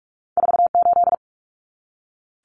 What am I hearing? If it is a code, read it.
Morse code, heard as a keyed tone, about 46 wpm, 700 Hz. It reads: 48
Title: Common Cut Numbers - 2-Digit 45wpm